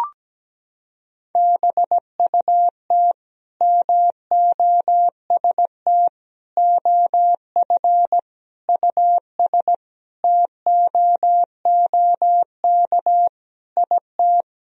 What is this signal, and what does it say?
Morse code, heard as a keyed tone, about 17 wpm, 700 Hz. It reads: BUT MOST OF US TOOK IT
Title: Sentences from Top 1000 Words - Review Part 1 - 17wpm